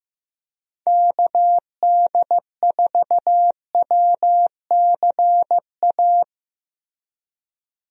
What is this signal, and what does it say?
Morse code, heard as a keyed tone, about 15 wpm, 700 Hz. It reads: KD4WCA
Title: USA Call Signs - 6 Characters - Repeated 2x - 15wpm